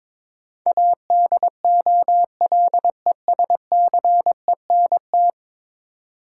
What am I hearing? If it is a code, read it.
Morse code, heard as a keyed tone, about 22 wpm, 700 Hz. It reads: ADOLESCENT
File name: practice / 22wpm / 10 Letter Words - First 3 Letters Given - 22wpm